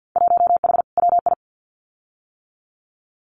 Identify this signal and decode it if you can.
Morse code, heard as a keyed tone, about 50 wpm, 700 Hz. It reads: 1H WI
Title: Winter Field Day Exchanges 50wpm